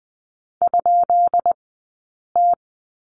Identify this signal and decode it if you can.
Morse code, heard as a keyed tone, about 19 wpm, 700 Hz. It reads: ? T
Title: 177 Lesson 29d - Slash - Two Characters 20wpm